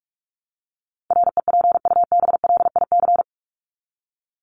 Morse code, heard as a keyed tone, about 44 wpm, 700 Hz. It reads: REPUBLIC
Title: Top 1251-1500 Words 45wpm